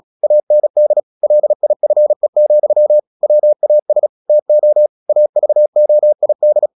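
Morse code, heard as a keyed tone, about 36 wpm, 600 Hz. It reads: AND LIFE, WAS TO AVOID
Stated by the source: Morse practice at 36 wpm, text of Ken Follett's novel Never, from CW med Gnister.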